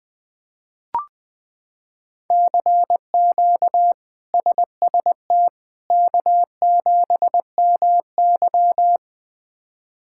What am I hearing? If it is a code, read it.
Morse code, heard as a keyed tone, about 20 wpm, 700 Hz. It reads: CQ SST K7MY
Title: K1USN SST QSO Practice - 20wpm